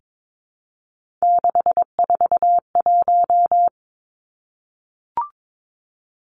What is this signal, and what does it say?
Morse code, heard as a keyed tone, about 22 wpm, 700 Hz. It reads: 641
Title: Three Challenging Letter-Number Combinations 22wpm